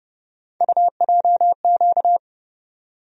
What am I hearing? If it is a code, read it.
Morse code, heard as a keyed tone, about 30 wpm, 700 Hz. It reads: UJQ
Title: USA Call Sign Postfixes - 3 Letters 30wpm